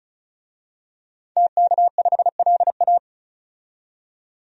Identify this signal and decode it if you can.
Morse code, heard as a keyed tone, about 34 wpm, 700 Hz. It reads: TK5LA
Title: DXCC-100 Call Signs - 5 Characters - 35wpm